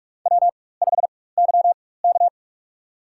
Morse code, heard as a keyed tone, about 44 wpm, spaced as for 21 wpm, 690 Hz. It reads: W5YK
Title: USA Call Signs - 4 Characters - 4x Character Spacing - 45wpm